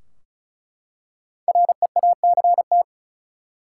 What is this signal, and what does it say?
Morse code, heard as a keyed tone, about 36 wpm, 700 Hz. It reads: REACT
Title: Five Letter Words - 35wpm